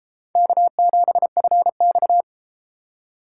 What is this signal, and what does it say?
Morse code, heard as a keyed tone, about 32 wpm, 700 Hz. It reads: K7FX